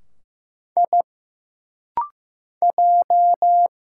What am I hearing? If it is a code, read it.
Morse code, heard as a keyed tone, about 15 wpm, 700 Hz. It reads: I J
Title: Single Letters - Rapid-Fire 15wpm